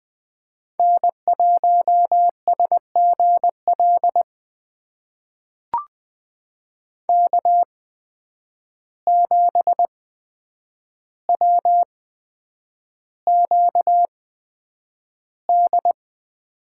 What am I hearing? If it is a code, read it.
Morse code, heard as a keyed tone, about 20 wpm, spaced as for 9 wpm, 700 Hz. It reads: N1SGL K7WQD
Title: USA Call Signs - 5 Characters - 8x Character Spacing - 20wpm